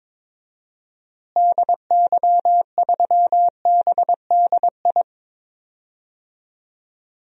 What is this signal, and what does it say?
Morse code, heard as a keyed tone, about 22 wpm, 700 Hz. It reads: DY3BDI
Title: DXCC-100 Call Signs - 6 Characters - Repeated 3x - 22wpm